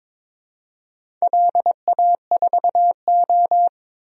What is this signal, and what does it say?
Morse code, heard as a keyed tone, about 22 wpm, 700 Hz. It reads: LA4O
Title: DXCC-100 Call Signs - 4 Characters - 22wpm